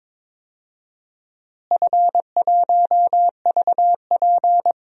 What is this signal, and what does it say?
Morse code, heard as a keyed tone, about 21 wpm, 700 Hz. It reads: F1VP
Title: DXCC-100 Call Signs - 4 Characters - Repeated 2x - 22wpm